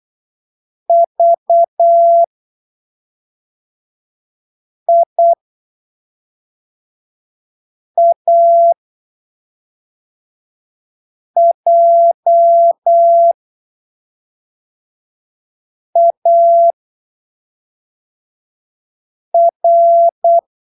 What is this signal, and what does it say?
Morse code, heard as a keyed tone, about 8 wpm, spaced as for 3 wpm, 680 Hz. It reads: VIAJAR
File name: arquivos / texto / texto 21 08ppm